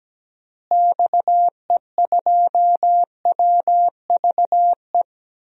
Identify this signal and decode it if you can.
Morse code, heard as a keyed tone, about 17 wpm, 700 Hz. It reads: XE2WVE
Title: DXCC-100 Call Signs - 6 Characters - Repeated 2x - 17wpm